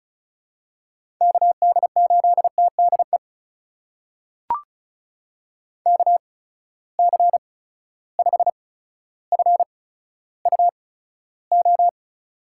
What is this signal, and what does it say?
Morse code, heard as a keyed tone, about 36 wpm, spaced as for 15 wpm, 700 Hz. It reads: KD8TDE KC5FUO
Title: USA Call Signs - 6 Characters - 8x Character Spacing - 35wpm